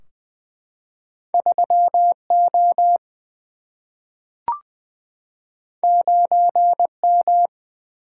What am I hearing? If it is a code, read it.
Morse code, heard as a keyed tone, about 20 wpm, 700 Hz. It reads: 3 O 9 M
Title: Single-Letter Single-Number 20wpm